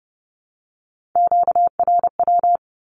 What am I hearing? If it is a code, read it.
Morse code, heard as a keyed tone, about 30 wpm, 700 Hz. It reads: QRW
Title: Q-Codes 30wpm